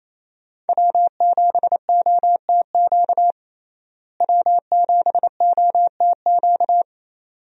Morse code, heard as a keyed tone, about 28 wpm, 700 Hz. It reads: W7OTQ W7OTQ
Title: USA Call Signs - 5 Characters - Repeated 2x - 28wpm